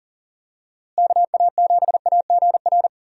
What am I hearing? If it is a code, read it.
Morse code, heard as a keyed tone, about 40 wpm, 700 Hz. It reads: KA7AGR
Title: USA Call Signs - 6 Characters - Repeated 2x - 40wpm